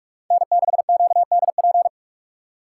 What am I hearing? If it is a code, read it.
Morse code, heard as a keyed tone, about 44 wpm, 700 Hz. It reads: N6QDP